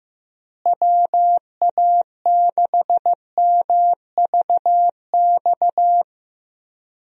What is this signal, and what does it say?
Morse code, heard as a keyed tone, about 15 wpm, 700 Hz. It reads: WA6MVX
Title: USA Call Signs - 6 Characters - Repeated 2x - 15wpm